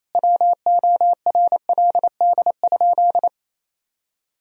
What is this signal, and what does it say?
Morse code, heard as a keyed tone, about 28 wpm, 700 Hz. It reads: WORLD?
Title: Sentences from Top 1000 Words - Review Part 1 - 28wpm